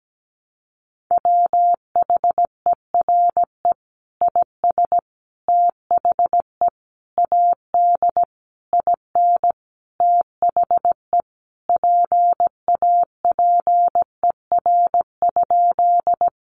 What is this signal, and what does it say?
Morse code, heard as a keyed tone, about 17 wpm, 700 Hz. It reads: WHERE IS THE AD IN THE PAPER?
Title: Sentences from Top 1300 Words 17wpm